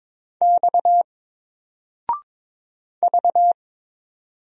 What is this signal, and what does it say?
Morse code, heard as a keyed tone, about 23 wpm, 700 Hz. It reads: X V
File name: Single-Letter-Number-22wpm